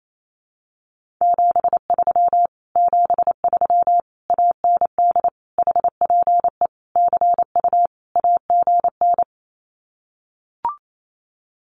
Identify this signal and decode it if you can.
Morse code, heard as a keyed tone, about 28 wpm, 700 Hz. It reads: 73 73 AND HPE CU AGN
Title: Common QSO Phrases Encore 28wpm